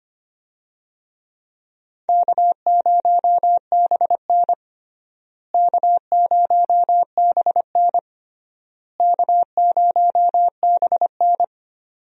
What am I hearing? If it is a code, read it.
Morse code, heard as a keyed tone, about 25 wpm, 700 Hz. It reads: K0BN K0BN K0BN
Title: USA Call Signs - 4 Characters - Repeated 3x - 25wpm